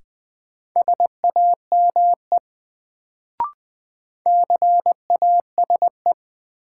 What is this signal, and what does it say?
Morse code, heard as a keyed tone, about 20 wpm, 700 Hz. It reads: SAME CASE
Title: Sets of 2 Words - New Word Drill - 20wpm